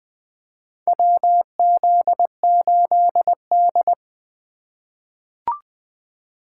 Morse code, heard as a keyed tone, about 21 wpm, 700 Hz. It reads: WZ8D
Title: USA Call Signs - 4 Characters - 4x Character Spacing - 20wpm